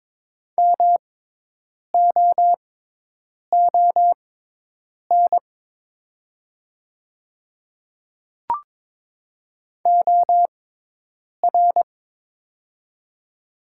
Morse code, heard as a keyed tone, about 22 wpm, spaced as for 8 wpm, 700 Hz. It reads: MOON OR
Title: 6x Character Spacing - Top 500 Words - 22wpm